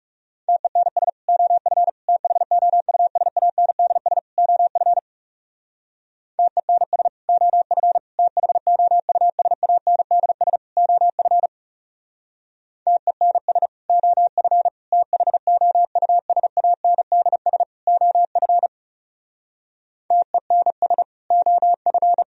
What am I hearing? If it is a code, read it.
Morse code, heard as a keyed tone, about 37 wpm, 700 Hz. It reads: TENS OF THOUSANDS OF TENS OF THOUSANDS OF TENS OF THOUSANDS OF TENS OF
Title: Speed-Racing - Sets of 4 Words 30wpm